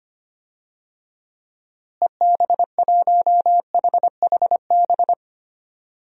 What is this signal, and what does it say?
Morse code, heard as a keyed tone, about 25 wpm, 700 Hz. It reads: EB1HHB
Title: DXCC-100 Call Signs - 6 Characters - Repeated 2x - 25wpm